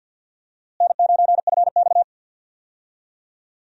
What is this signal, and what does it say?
Morse code, heard as a keyed tone, about 51 wpm, 700 Hz. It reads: N9FX
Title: USA Call Signs - 4 Characters - Repeated 2x - 50wpm